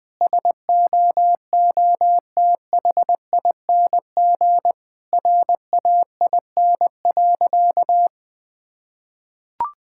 Morse code, heard as a keyed tone, about 20 wpm, 700 Hz. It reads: SOOTHING RAIN.
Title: Difficult Sentences with Hints - 20wpm